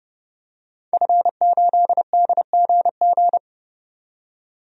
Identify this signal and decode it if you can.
Morse code, heard as a keyed tone, about 30 wpm, 700 Hz. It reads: F8DGG
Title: DXCC-100 Call Signs - 5 Characters - 30wpm